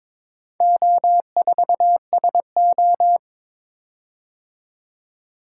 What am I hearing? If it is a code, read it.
Morse code, heard as a keyed tone, about 22 wpm, 700 Hz. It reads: O4SO
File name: practice / 22wpm / Four Letter-Number Combinations - 22wpm